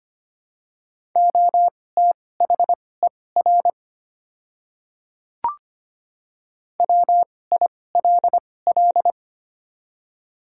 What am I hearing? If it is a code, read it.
Morse code, heard as a keyed tone, about 25 wpm, spaced as for 18 wpm, 700 Hz. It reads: OTHER WILL
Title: 2x Character Spacing - Top 100 Words - 25wpm